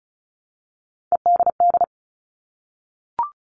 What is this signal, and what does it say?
Morse code, heard as a keyed tone, about 34 wpm, 700 Hz. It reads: EDD
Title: All Cut Numbers - 3-Digit 35wpm